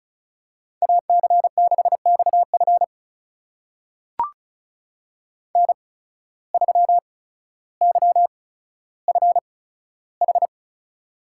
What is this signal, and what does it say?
Morse code, heard as a keyed tone, about 36 wpm, spaced as for 15 wpm, 700 Hz. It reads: AC6XF N3YFH
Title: USA Call Signs - 5 Characters - 8x Character Spacing - 35wpm